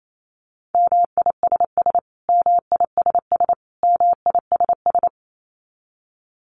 Morse code, heard as a keyed tone, about 28 wpm, 700 Hz. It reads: MISS MISS MISS
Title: Top 500 Words Repeated 28wpm